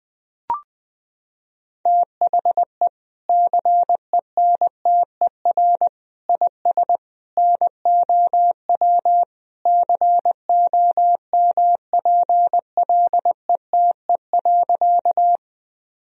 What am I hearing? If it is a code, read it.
Morse code, heard as a keyed tone, about 20 wpm, 700 Hz. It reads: THE CENTER IS NOW COMPLETE.
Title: Sentences from Top 1000 Words - Review Part 2 - 20wpm